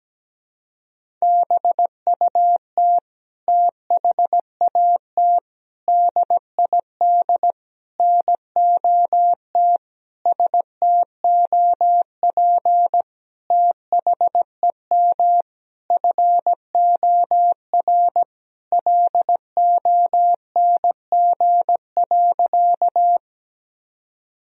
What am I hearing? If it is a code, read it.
Morse code, heard as a keyed tone, about 17 wpm, 700 Hz. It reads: BUT THAT DID NOT STOP THEM FOR LONG.
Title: Sentences from Top 1000 Words - Review Part 1 - 17wpm